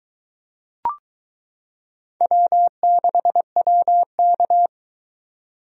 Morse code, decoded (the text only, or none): W6WK